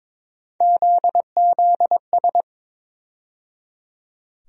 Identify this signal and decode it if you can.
Morse code, heard as a keyed tone, about 22 wpm, 700 Hz. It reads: ZZS
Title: USA Call Sign Postfixes - 3 Letters 22wpm